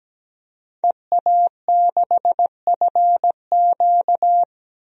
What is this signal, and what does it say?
Morse code, heard as a keyed tone, about 17 wpm, 700 Hz. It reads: EA6FQ
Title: DXCC-100 Call Signs - 5 Characters - 17wpm